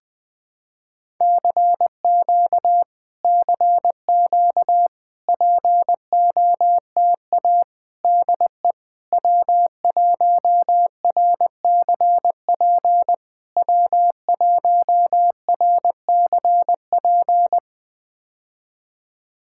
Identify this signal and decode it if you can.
Morse code, heard as a keyed tone, about 20 wpm, 700 Hz. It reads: CQ CQ POTA DE W1RCP W1RCP
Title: POTA QSO Practice - 20wpm